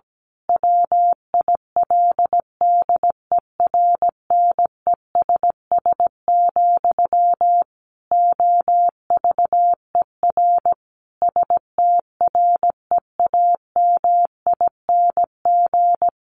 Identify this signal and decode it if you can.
Morse code, heard as a keyed tone, about 17 wpm, 700 Hz. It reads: WILDERNESS, OVER STREAMING
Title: A Sound Of Thunder 17wpm